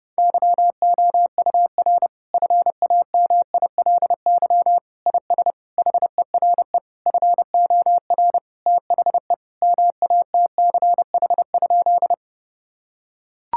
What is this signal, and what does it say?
Morse code, heard as a keyed tone, about 30 wpm, 700 Hz. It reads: YOUR FAMILY IS HERE FOR THE MATCH?
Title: Sentences from Top 1600 Words 30wpm